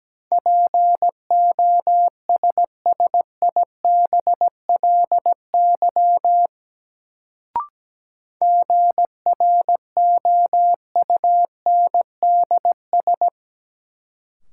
Morse code, heard as a keyed tone, about 17 wpm, 700 Hz. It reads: POSSIBLY GROUNDS
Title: Sentences from Top 1300 Words - New Word Drill - 17wpm